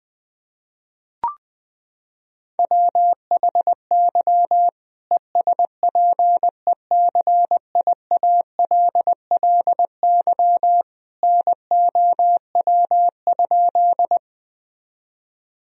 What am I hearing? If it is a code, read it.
Morse code, heard as a keyed tone, about 20 wpm, 700 Hz. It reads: WHY ESPECIALLY NOW?
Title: Sentences from Top 1000 Words - Review Part 4 - 20wpm